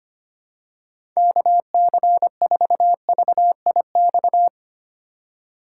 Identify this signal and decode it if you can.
Morse code, heard as a keyed tone, about 25 wpm, 700 Hz. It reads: KC4VIX
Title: USA Call Signs - 6 Characters - 2x Character Spacing - 25wpm